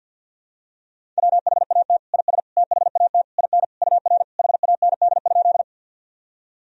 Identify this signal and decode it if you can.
Morse code, heard as a keyed tone, about 50 wpm, 700 Hz. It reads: WHAT IS THAT IN UR HAND?
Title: Sentences from Top 200 Words and Abbreviations 50wpm